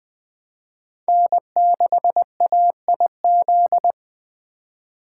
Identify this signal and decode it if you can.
Morse code, heard as a keyed tone, about 20 wpm, 700 Hz. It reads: N6AIZ